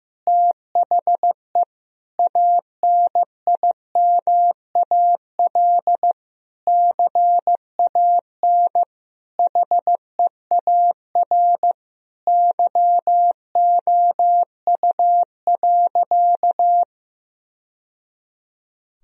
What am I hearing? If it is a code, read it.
Morse code, heard as a keyed tone, about 15 wpm, 700 Hz. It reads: THE ANIMAL CAN HEAR YOU.
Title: Sentences from Top 1000 Words - Review Part 1 - 15wpm